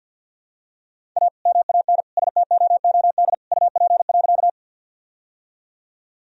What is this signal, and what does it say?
Morse code, heard as a keyed tone, about 46 wpm, 700 Hz. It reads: A MAN STOOD UP.
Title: Sentences from Top 1000 Words - Review Part 2 - 50wpm